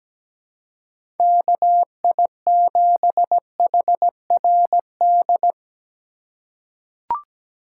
Morse code, heard as a keyed tone, about 17 wpm, 700 Hz. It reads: KI7HRD
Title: USA Call Signs - 6 Characters - 2x Character Spacing - 17wpm